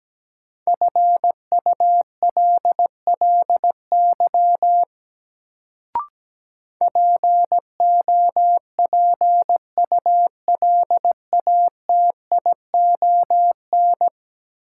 Morse code, heard as a keyed tone, about 17 wpm, 700 Hz. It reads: FULLY POPULATION
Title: Sentences from Top 1300 Words - New Word Drill Alt - 17wpm